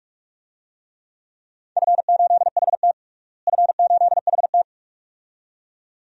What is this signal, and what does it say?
Morse code, heard as a keyed tone, about 42 wpm, 700 Hz. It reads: F8HT F8HT
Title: DXCC-100 Call Signs - 4 Characters - Repeated 2x - 45wpm